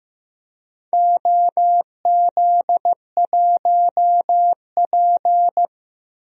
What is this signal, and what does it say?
Morse code, heard as a keyed tone, about 15 wpm, 700 Hz. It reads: OZ1P
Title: DXCC-100 Call Signs - 4 Characters - 15wpm